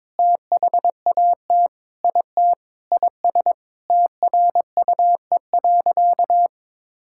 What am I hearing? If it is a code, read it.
Morse code, heard as a keyed tone, about 22 wpm, 700 Hz. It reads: THAT IT IS TRUE.
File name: practice / 22wpm / Sentences with 4-Word Phrases - 22wpm